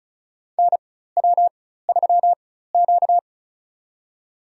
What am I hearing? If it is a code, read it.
Morse code, heard as a keyed tone, about 36 wpm, spaced as for 16 wpm, 700 Hz. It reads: NW3Q
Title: USA Call Signs - 4 Characters - 4x Character Spacing - 35wpm